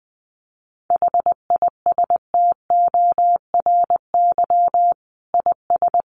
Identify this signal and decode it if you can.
Morse code, heard as a keyed tone, about 20 wpm, 700 Hz. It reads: HISTORY IS
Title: Sentences from Top 600 Words 20wpm